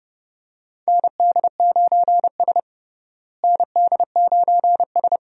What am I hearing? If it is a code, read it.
Morse code, heard as a keyed tone, about 32 wpm, 700 Hz. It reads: ND9S ND9S
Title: USA Call Signs - 4 Characters - Repeated 3x - 30wpm